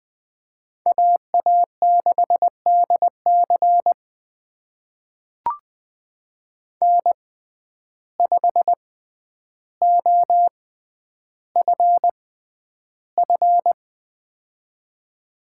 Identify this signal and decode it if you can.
Morse code, heard as a keyed tone, about 20 wpm, spaced as for 10 wpm, 700 Hz. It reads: AA6DC N5OFF
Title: USA Call Signs - 6x Character Spacing - 20wpm